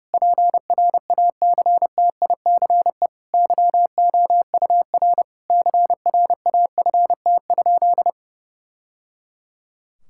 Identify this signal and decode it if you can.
Morse code, heard as a keyed tone, about 30 wpm, 700 Hz. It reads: PRACTICE YOUR CRAFT?
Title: Sentences from Top 1000 Words - Review Part 6 - 30wpm